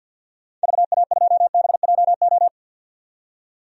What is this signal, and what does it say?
Morse code, heard as a keyed tone, about 52 wpm, 700 Hz. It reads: VA2BJO
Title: DXCC-100 Call Signs - 6 Characters - 50wpm